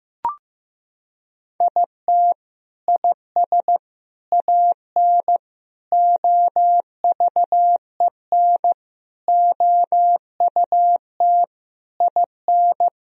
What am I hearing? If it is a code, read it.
Morse code, heard as a keyed tone, about 14 wpm, 700 Hz. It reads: IT IS AN OVEN OUT IN
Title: Sentences from Top 1000 Words - Review Part 4 - 15wpm